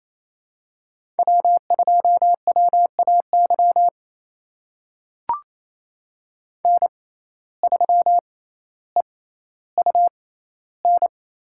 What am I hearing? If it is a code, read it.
Morse code, heard as a keyed tone, about 28 wpm, spaced as for 14 wpm, 700 Hz. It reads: W2WAY N3EUN
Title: USA Call Signs - 5 Characters - 6x Character Spacing - 28wpm